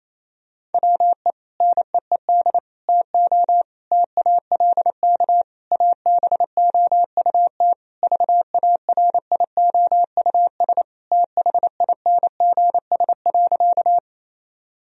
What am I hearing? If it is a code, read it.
Morse code, heard as a keyed tone, about 28 wpm, 700 Hz. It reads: WE NEED TO TALK ABOUT VARIOUS THINGS.
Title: Sentences from Top 1000 Words - Review Part 4 - 28wpm